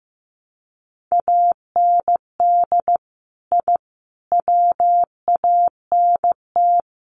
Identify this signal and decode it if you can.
Morse code, heard as a keyed tone, about 15 wpm, 700 Hz. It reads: AND I WANT
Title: Sets of 4 Words 15wpm